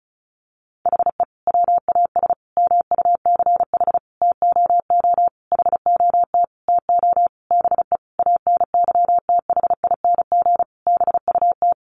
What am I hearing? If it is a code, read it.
Morse code, heard as a keyed tone, about 34 wpm, 700 Hz. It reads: HE WAS MUCH TOO HOT TO BE ANYTHING BUT
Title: Sentences from Top 1200 Words 35wpm